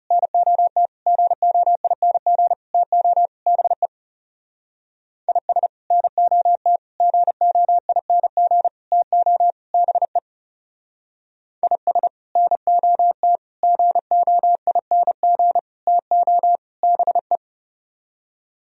Morse code, heard as a keyed tone, about 36 wpm, 700 Hz. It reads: NOT GOING TO BE IS NOT GOING TO BE IS NOT GOING TO BE
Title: Speed-Racing - Sets of 5 Words 30wpm